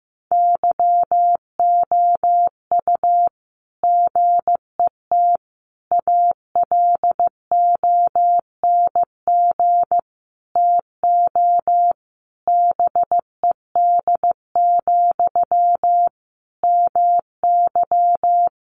Morse code, heard as a keyed tone, about 15 wpm, 700 Hz. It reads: YOU GET ALONG TO BED, MY
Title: Sentences from Top 700 Words 15wpm